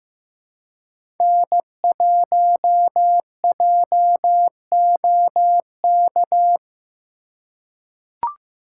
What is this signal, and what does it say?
Morse code, heard as a keyed tone, about 15 wpm, 700 Hz. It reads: N1JOK
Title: USA Call Signs - 5 Characters - 2x Character Spacing - 15wpm